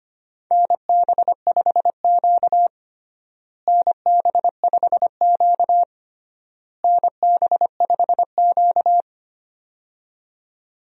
Morse code, heard as a keyed tone, about 25 wpm, 700 Hz. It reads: NB5Q NB5Q NB5Q
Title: USA Call Signs - 4 Characters - Repeated 3x - 25wpm